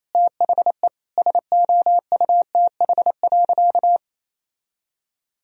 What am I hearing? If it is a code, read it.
Morse code, heard as a keyed tone, about 28 wpm, 700 Hz. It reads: THE SOUTH.
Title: Sentences from Top 1000 Words - Review Part 2 - 28wpm